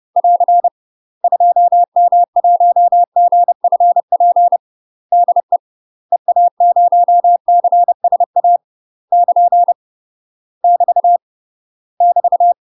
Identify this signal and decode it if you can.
Morse code, heard as a keyed tone, about 30 wpm, 700 Hz. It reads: <AR> 2M1GFP DE EA0CSA <KN> <BT> <BT>
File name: UK MASTER QSOs TRAINING/30WPM